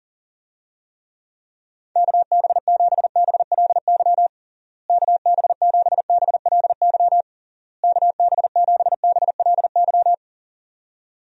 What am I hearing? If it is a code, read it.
Morse code, heard as a keyed tone, about 40 wpm, 700 Hz. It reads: KB7BLY KB7BLY KB7BLY
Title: USA Call Signs - 6 Characters - Repeated 3x - 40wpm